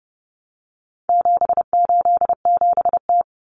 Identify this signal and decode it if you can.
Morse code, heard as a keyed tone, about 30 wpm, 700 Hz. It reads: 787T
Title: Common Cut Numbers - 4-Digit 30wpm